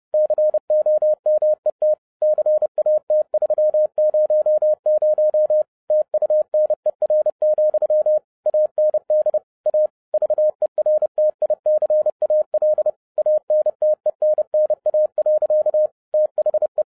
Morse code, heard as a keyed tone, about 30 wpm, 610 Hz. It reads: COMET CAT300 TUNER, AND A VERTICAL ANTENNA. THE
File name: 2012-01-mp3 / n11-I DIDNT MISS THE SOLAR CYCLE 24-30wpm